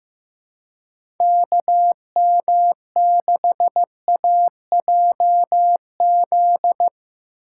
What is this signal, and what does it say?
Morse code, heard as a keyed tone, about 15 wpm, 700 Hz. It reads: KM6AJZ
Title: USA Call Signs - 6x Character Spacing - 15wpm